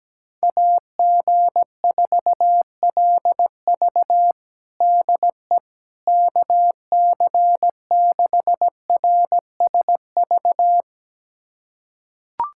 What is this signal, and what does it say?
Morse code, heard as a keyed tone, about 17 wpm, 700 Hz. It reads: AG4LV DE KC6RSV
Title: USA Callsign DE Callsign 17wpm